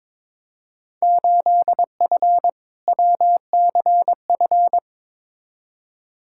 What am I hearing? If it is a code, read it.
Morse code, heard as a keyed tone, about 22 wpm, 700 Hz. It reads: 8F WCF
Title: Field Day Exchanges Encore - 22wpm